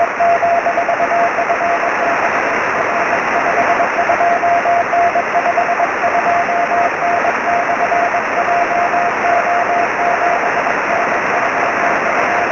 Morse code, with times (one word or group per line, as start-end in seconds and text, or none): none